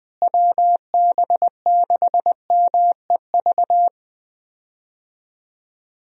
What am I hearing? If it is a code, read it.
Morse code, heard as a keyed tone, about 21 wpm, 700 Hz. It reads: WB6MEV
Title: USA Call Signs - 6 Characters - Repeated 3x - 20wpm